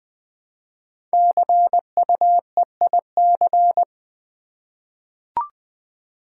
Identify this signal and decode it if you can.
Morse code, heard as a keyed tone, about 20 wpm, 700 Hz. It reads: CUEIC
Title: Code Groups - 5 Letter:Numbers - 20wpm